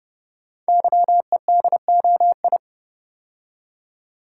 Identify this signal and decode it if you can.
Morse code, heard as a keyed tone, about 30 wpm, 700 Hz. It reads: YEDOI
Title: Code Groups - 5 Letters - 30wpm